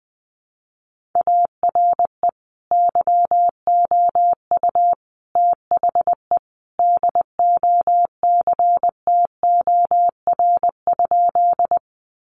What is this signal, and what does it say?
Morse code, heard as a keyed tone, about 20 wpm, 700 Hz. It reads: ARE YOU THE DOCTOR?
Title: Sentences from Top 1500 Words 20wpm